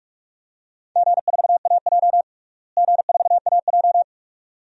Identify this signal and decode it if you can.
Morse code, heard as a keyed tone, about 46 wpm, 700 Hz. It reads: G4AJ G4AJ
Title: DXCC-100 Call Signs - 4 Characters - Repeated 2x - 45wpm